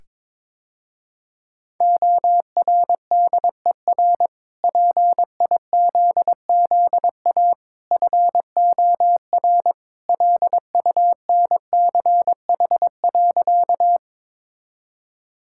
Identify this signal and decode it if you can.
Morse code, heard as a keyed tone, about 22 wpm, 700 Hz. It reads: ORDER PIZZA FOR LUNCH.
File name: practice / 22wpm / Easy Sentences with Hints - 22wpm